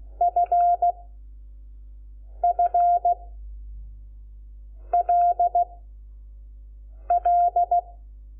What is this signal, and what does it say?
Morse code, heard as a keyed tone, about 16 wpm, 700 Hz. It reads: F F L L
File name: Lecciones / CLASE Nº5